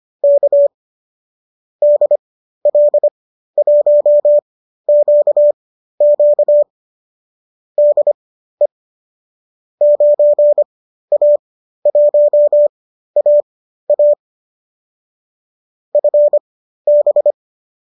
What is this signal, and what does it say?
Morse code, heard as a keyed tone, about 25 wpm, spaced as for 13 wpm, 590 Hz. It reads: K DL1QQ DE 9A1AA FB